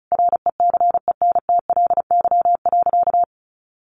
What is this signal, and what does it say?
Morse code, heard as a keyed tone, about 34 wpm, 700 Hz. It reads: RECENTLY.
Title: Sentences from Top 1200 Words 35wpm